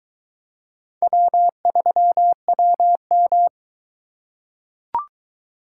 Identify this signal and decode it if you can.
Morse code, heard as a keyed tone, about 22 wpm, 700 Hz. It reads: W3WM